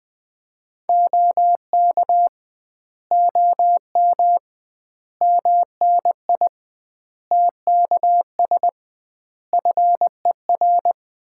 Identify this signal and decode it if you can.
Morse code, heard as a keyed tone, about 21 wpm, 700 Hz. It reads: OK OM MNI TKS FER
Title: Common QSO Phrases - 2x Word-Spacing - 20wpm